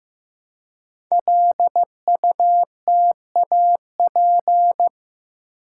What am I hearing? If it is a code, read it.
Morse code, heard as a keyed tone, about 15 wpm, 700 Hz. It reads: LUTAP
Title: Code Groups - 5 Letters - 15wpm